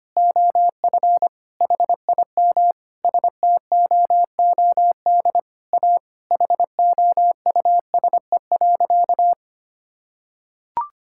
Morse code, heard as a keyed tone, about 25 wpm, 700 Hz. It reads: OF HIM STOOD A HOUSE.
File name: practice / 25wpm / Sentences with 4-Word Phrases - 25wpm